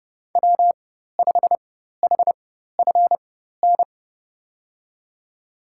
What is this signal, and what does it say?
Morse code, heard as a keyed tone, about 30 wpm, spaced as for 14 wpm, 700 Hz. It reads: W5HFN
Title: USA Call Signs - 5 Characters - 4x Character Spacing - 30wpm